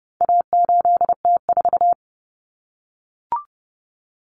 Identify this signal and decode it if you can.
Morse code, heard as a keyed tone, about 30 wpm, 700 Hz. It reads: A8T4
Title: Common Cut Numbers - 4-Digit 30wpm